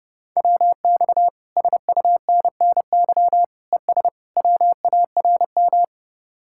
Morse code, heard as a keyed tone, about 30 wpm, 700 Hz. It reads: WX SUNNY ES WARM